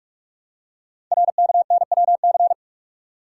Morse code, heard as a keyed tone, about 44 wpm, 700 Hz. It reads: RKNWC
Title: Code Groups - 5 Letters - 45wpm